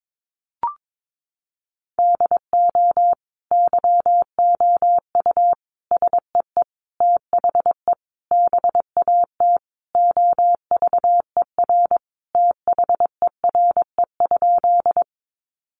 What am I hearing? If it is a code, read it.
Morse code, heard as a keyed tone, about 22 wpm, 700 Hz. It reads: DO YOU SEE THE BAT OVER THERE?
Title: Sentences from Top 1500 Words 22wpm